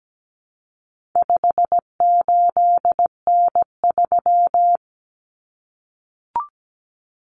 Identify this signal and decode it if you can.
Morse code, heard as a keyed tone, about 17 wpm, 700 Hz. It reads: 58N3
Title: Common Cut Numbers - 4-Digit 17wpm